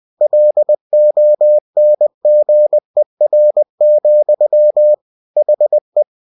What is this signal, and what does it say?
Morse code, heard as a keyed tone, about 20 wpm, 600 Hz. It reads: LONGER, HE
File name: cw-books / winnie-the-pooh / winnie-20wpm-page-0076